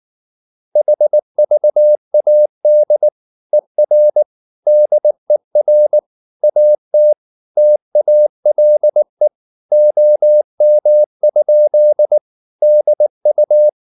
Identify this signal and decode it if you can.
Morse code, heard as a keyed tone, about 19 wpm, 600 Hz. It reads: HVAD ER DER AT TALE OM? DU
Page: Ferie 19 wpm